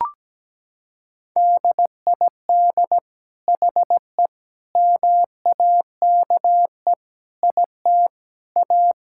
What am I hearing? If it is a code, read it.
Morse code, heard as a keyed tone, about 17 wpm, 700 Hz. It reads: DID HE MAKE IT A
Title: Sentences from Top 1600 Words 17wpm